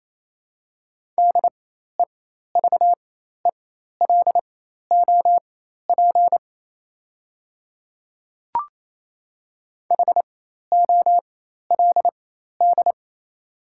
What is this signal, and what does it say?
Morse code, heard as a keyed tone, about 28 wpm, spaced as for 13 wpm, 700 Hz. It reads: DEVELOP HOLD
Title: 4x Character Spacing - Top 500 Words - 28wpm